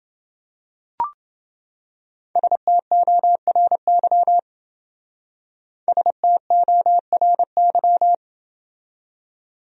Morse code, heard as a keyed tone, about 29 wpm, 700 Hz. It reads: STORY STORY